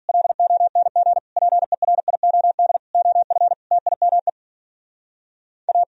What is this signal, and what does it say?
Morse code, heard as a keyed tone, about 50 wpm, 700 Hz. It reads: LONG PERIOD OF TIME A